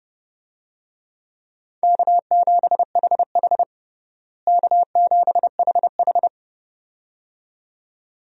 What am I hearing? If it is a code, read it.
Morse code, heard as a keyed tone, about 32 wpm, 700 Hz. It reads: K7HH K7HH
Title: USA Call Signs - 4 Characters - Repeated 2x - 30wpm